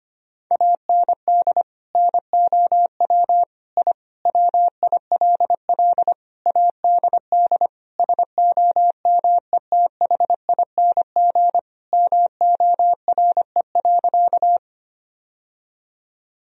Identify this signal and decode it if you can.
Morse code, heard as a keyed tone, about 25 wpm, 700 Hz. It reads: AND NOW I WILL ADD SOMETHING MORE.
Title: Sentences from Top 1000 Words - Review Part 1 - 25wpm